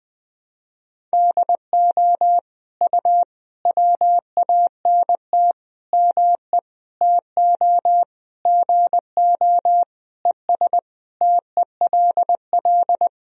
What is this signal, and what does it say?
Morse code, heard as a keyed tone, about 20 wpm, 700 Hz. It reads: DO U WANT ME TO GO ES TELL
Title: Sentences from Top 200 Words and Abbreviations 20wpm